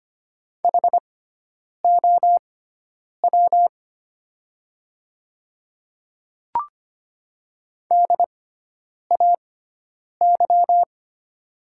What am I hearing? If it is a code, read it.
Morse code, heard as a keyed tone, about 25 wpm, spaced as for 9 wpm, 700 Hz. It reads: HOW DAY
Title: 6x Character Spacing - Top 100 Words - 25wpm